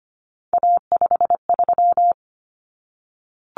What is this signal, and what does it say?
Morse code, heard as a keyed tone, about 25 wpm, 700 Hz. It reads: A53
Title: Common Cut Numbers - 3-Digit 25wpm